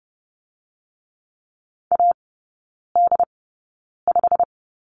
Morse code, heard as a keyed tone, about 30 wpm, spaced as for 9 wpm, 700 Hz. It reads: AD5
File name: Lesson 16e - Letter c - Three Characters 30wpm